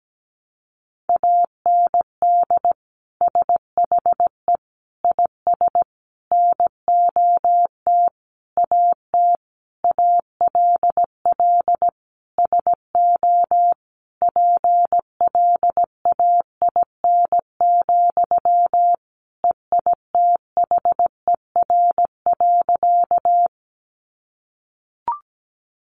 Morse code, heard as a keyed tone, about 17 wpm, 700 Hz. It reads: AND SHE IS NOT AT ALL SO PLAIN, EITHER.
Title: Sentences from Top 600 Words 17wpm